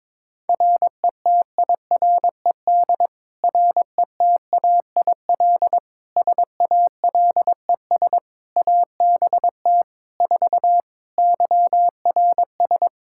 Morse code, heard as a keyed tone, about 22 wpm, 700 Hz. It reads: RETIRED RETAIL SALES ABT 4 YRS
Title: Practice QSO - Exchange 3 - More Info - 22wpm